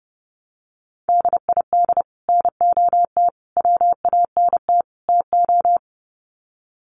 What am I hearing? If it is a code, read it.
Morse code, heard as a keyed tone, about 30 wpm, 700 Hz. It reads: DID NOT WANT TO
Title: Sets of 4 Words 30wpm